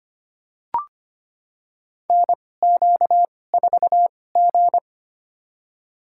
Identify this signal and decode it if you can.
Morse code, heard as a keyed tone, about 25 wpm, spaced as for 18 wpm, 700 Hz. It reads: NQ4G
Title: USA Call Signs - 2x Character Spacing - 25wpm